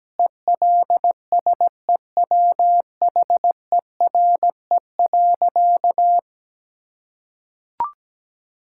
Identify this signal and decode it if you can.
Morse code, heard as a keyed tone, about 17 wpm, 700 Hz. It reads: ELSEWHERE.
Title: Sentences from Top 1600 Words 17wpm